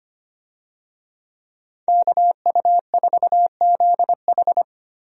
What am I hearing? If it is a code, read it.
Morse code, heard as a keyed tone, about 25 wpm, 700 Hz. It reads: KU4ZH
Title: USA Call Signs - 5 Characters - Repeated 3x - 25wpm